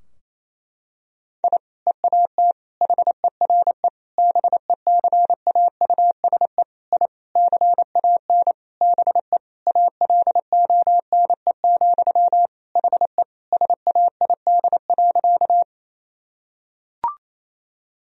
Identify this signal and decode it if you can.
Morse code, heard as a keyed tone, about 28 wpm, 700 Hz. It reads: I EAT HERE BECAUSE I CAN BE ALONE, HE SAID.
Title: Sentences from Top 1000 Words - Review Part 3 - 28wpm